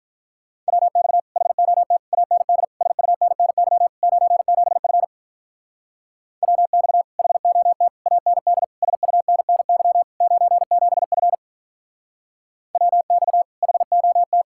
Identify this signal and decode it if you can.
Morse code, heard as a keyed tone, about 44 wpm, 700 Hz. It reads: WX HOT AND SUNNY 97F WX HOT AND SUNNY 97F WX HOT
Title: Speed-Racing Common QSO Phrases Encore - 35wpm